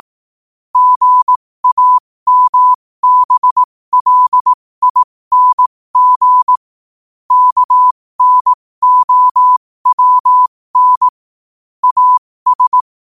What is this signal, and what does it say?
Morse code, heard as a keyed tone, about 18 wpm, spaced as for 15 wpm, 1000 Hz. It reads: GAMBLING KNOWN AS
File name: qotdfiles / 15wpm